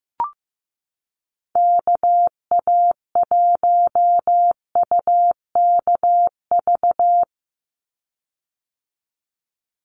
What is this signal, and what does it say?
Morse code, heard as a keyed tone, about 15 wpm, 700 Hz. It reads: KA1UKV
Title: USA Call Signs 15wpm